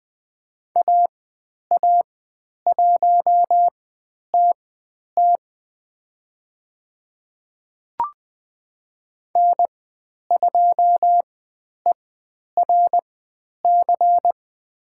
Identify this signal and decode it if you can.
Morse code, heard as a keyed tone, about 20 wpm, spaced as for 10 wpm, 700 Hz. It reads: AA1TT N2ERC
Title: 102 Lesson 17g - Letter u - Call Signs 20wpm